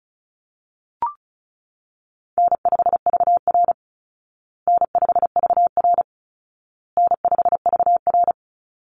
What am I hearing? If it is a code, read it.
Morse code, heard as a keyed tone, about 34 wpm, 700 Hz. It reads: N5VR N5VR N5VR
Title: Easy Call Signs Repeated 3x - 35wpm